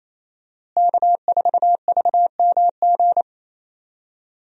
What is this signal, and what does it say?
Morse code, heard as a keyed tone, about 28 wpm, 700 Hz. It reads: K4VMG
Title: USA Call Signs - 5 Characters - Repeated 2x - 28wpm